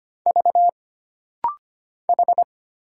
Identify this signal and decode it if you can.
Morse code, heard as a keyed tone, about 25 wpm, 700 Hz. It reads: V H
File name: Single Character (Letter:Number:Punctuation:Prosign) - Rapid-Fire - 25wpm